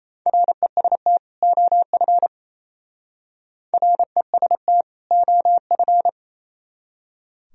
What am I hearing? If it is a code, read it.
Morse code, heard as a keyed tone, about 36 wpm, 700 Hz. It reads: REST OF REST OF